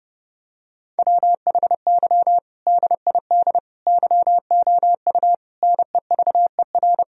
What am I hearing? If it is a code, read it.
Morse code, heard as a keyed tone, about 32 wpm, 700 Hz. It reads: WHY DID YOU NEVER
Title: Sentences from Top 1000 Words - Review Part 1 - 30wpm